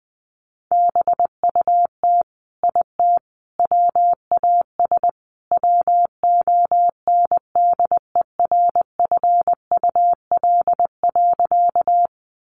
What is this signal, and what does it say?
Morse code, heard as a keyed tone, about 20 wpm, 700 Hz. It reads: BUT IT WAS WONDERFUL.
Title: Sentences from Top 1100 Words 20wpm